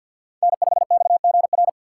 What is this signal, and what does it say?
Morse code, heard as a keyed tone, about 52 wpm, 690 Hz. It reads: N5XGR
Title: USA Call Signs - 5 Characters - 2x Character Spacing - 50wpm